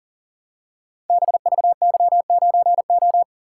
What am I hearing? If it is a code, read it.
Morse code, heard as a keyed tone, about 38 wpm, 700 Hz. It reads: BVY9O
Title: Code Groups - 5 Letter:Numbers - 40wpm